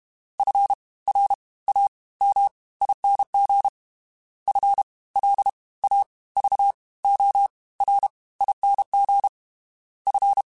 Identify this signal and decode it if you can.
Morse code, heard as a keyed tone, about 32 wpm, 800 Hz. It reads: F R A M ING F L A V O R ING F
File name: Special INGs / Special INGs-mp3s / Fast ing @30